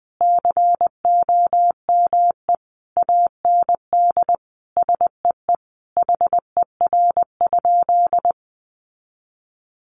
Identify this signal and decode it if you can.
Morse code, heard as a keyed tone, about 20 wpm, 700 Hz. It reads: COME AND SEE HER?
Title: Sentences from Top 100 Words 20wpm